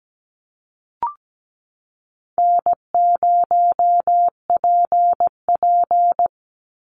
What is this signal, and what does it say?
Morse code, heard as a keyed tone, about 17 wpm, 700 Hz. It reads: N0PP
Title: Easy Call Signs - 17wpm